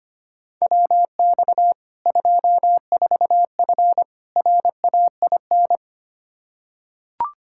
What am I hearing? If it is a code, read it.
Morse code, heard as a keyed tone, about 25 wpm, 700 Hz. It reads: WX 24F RAIN
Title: K1USN SST QSO Practice 25wpm